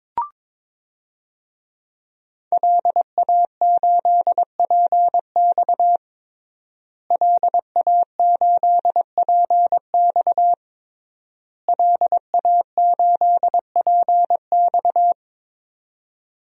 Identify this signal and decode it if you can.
Morse code, heard as a keyed tone, about 22 wpm, 700 Hz. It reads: LA8PX LA8PX LA8PX
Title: DXCC-100 Call Signs - 5 Characters - Repeated 3x - 22wpm